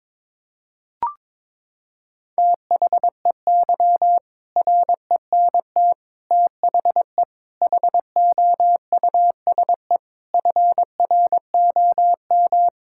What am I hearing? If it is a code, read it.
Morse code, heard as a keyed tone, about 21 wpm, 700 Hz. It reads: THEY RENT THE HOUSE FROM
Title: Sentences from Top 1000 Words - Review Part 5 - 22wpm